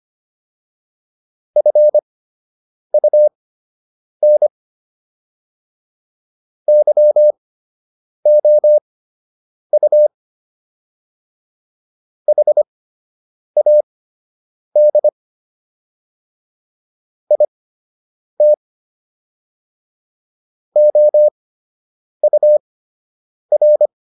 Morse code, heard as a keyed tone, about 25 wpm, spaced as for 8 wpm, 600 Hz. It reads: FUN YOU HAD IT OUR